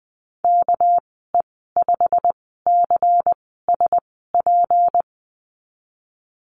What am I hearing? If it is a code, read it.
Morse code, heard as a keyed tone, about 20 wpm, spaced as for 15 wpm, 700 Hz. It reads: KE5CSP
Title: 207 Lesson 33g - Letter j - Call Signs 20wpm